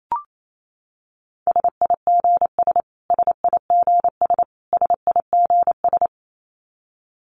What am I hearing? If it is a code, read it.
Morse code, heard as a keyed tone, about 28 wpm, 700 Hz. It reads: SIGS SIGS SIGS
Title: Abbreviations Repeated 3x - 28wpm